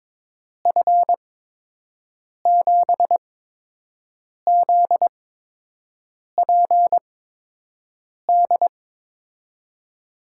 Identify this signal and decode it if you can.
Morse code, heard as a keyed tone, about 22 wpm, spaced as for 6 wpm, 700 Hz. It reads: F7ZPD
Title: USA Call Signs - 6 Characters - 8x Character Spacing - 22wpm